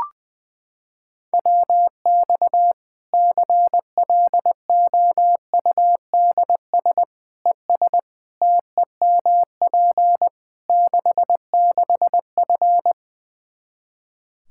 Morse code, heard as a keyed tone, about 20 wpm, 700 Hz. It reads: WX CLOUDS ES TEMP 66F
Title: Practice QSO - Exchange 2 - Info - 20wpm